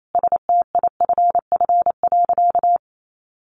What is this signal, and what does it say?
Morse code, heard as a keyed tone, about 28 wpm, 700 Hz. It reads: STIFF.
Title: A Sound Of Thunder 28wpm